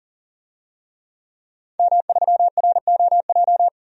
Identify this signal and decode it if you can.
Morse code, heard as a keyed tone, about 40 wpm, 700 Hz. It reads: M3ROJ
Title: DXCC-100 Call Signs - 5 Characters - 40wpm